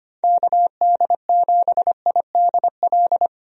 Morse code, heard as a keyed tone, about 25 wpm, 700 Hz. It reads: KD7IDL
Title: USA Call Signs - 6 Characters - Repeated 2x - 25wpm